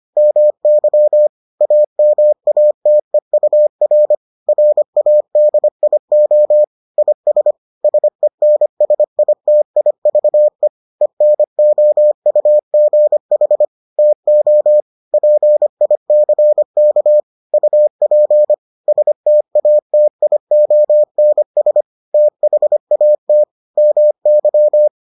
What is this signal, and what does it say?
Morse code, heard as a keyed tone, about 25 wpm, 600 Hz. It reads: MY AMATEUR RADIO IS SENSITIVE ENOUGH TO PICK UP STATIONS THAT MY